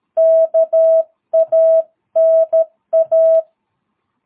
Morse code, heard as a keyed tone, about 13 wpm, 660 Hz. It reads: KANA
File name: jh8cbh / 2022cw